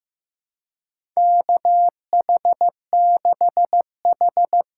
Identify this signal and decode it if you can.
Morse code, heard as a keyed tone, about 15 wpm, 700 Hz. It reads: KH6H
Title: USA Call Signs - 3x Character Spacing - 15wpm